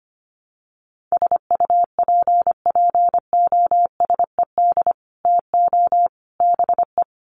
Morse code, heard as a keyed tone, about 25 wpm, 700 Hz. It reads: SUPPOSED TO BE
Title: Sets of 3 Words 25wpm